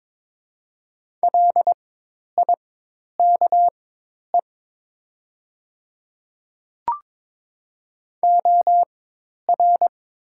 Morse code, heard as a keyed tone, about 22 wpm, spaced as for 10 wpm, 700 Hz. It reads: LIKE OR
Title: 4x Character Spacing - Top 100 Words - 22wpm